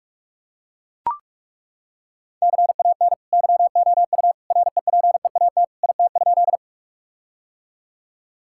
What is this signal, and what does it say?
Morse code, heard as a keyed tone, about 41 wpm, 700 Hz. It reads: CAN YOU REPEAT IT?
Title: Sentences from Top 1600 Words 45wpm